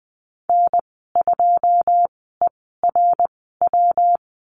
Morse code, heard as a keyed tone, about 20 wpm, spaced as for 14 wpm, 700 Hz. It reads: N2ERW
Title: 193 Lesson 31g - Letter v - Call Signs 20wpm